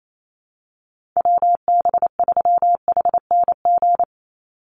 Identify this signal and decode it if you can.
Morse code, heard as a keyed tone, about 28 wpm, 700 Hz. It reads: WB3HNG
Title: USA Call Signs 28wpm